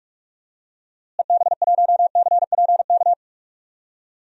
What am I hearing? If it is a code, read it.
Morse code, heard as a keyed tone, about 42 wpm, 700 Hz. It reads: EB1CPK
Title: DXCC-100 Call Signs - 6 Characters - Repeated 3x - 45wpm